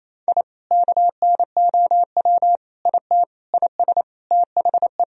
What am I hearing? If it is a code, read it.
Morse code, heard as a keyed tone, about 28 wpm, 700 Hz. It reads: I KNOW IT IS THE
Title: Sentences from Top 1000 Words - Review Part 3 - 28wpm